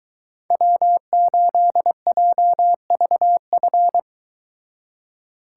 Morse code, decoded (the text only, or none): W8JVF